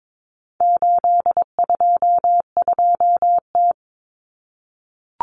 Morse code, heard as a keyed tone, about 22 wpm, 700 Hz. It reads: 822T
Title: Common Cut Numbers - 4-Digit 22wpm